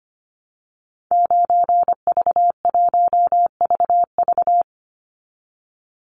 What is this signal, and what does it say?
Morse code, heard as a keyed tone, about 25 wpm, 700 Hz. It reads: 9V1VV
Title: Call Signs Repeated 2x - 25wpm